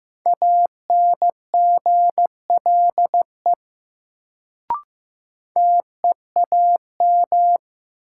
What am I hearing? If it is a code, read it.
Morse code, heard as a keyed tone, about 15 wpm, 700 Hz. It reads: ANGLE TEAM
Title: Sentences from Top 1500 Words - New Word Drill - 15wpm